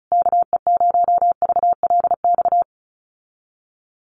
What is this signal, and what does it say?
Morse code, heard as a keyed tone, about 34 wpm, 700 Hz. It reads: KE0VLX
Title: USA Call Signs Repeated 3x 35wpm